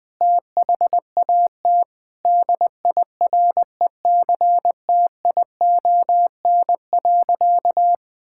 Morse code, heard as a keyed tone, about 21 wpm, 700 Hz. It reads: THAT DIRECTION.
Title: Sentences from Top 1000 Words - Review Part 5 - 20wpm